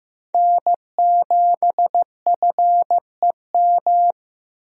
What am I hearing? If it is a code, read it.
Morse code, heard as a keyed tone, about 15 wpm, 700 Hz. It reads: N7FEM
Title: USA Call Signs - 5 Characters - 15wpm